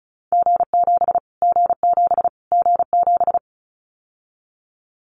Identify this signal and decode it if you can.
Morse code, heard as a keyed tone, about 34 wpm, 700 Hz. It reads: G7 G7 G7
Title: Callsign Prefixes for Top 25 DXCC - Repeated 3x - 35wpm